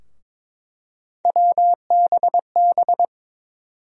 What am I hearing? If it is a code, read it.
Morse code, heard as a keyed tone, about 21 wpm, 700 Hz. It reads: WBB
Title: USA Call Sign Postfixes - 3 Letters 22wpm